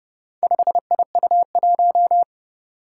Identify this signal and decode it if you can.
Morse code, heard as a keyed tone, about 30 wpm, 700 Hz. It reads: 5IU1
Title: Four Challenging Letter-Number Combinations 30wpm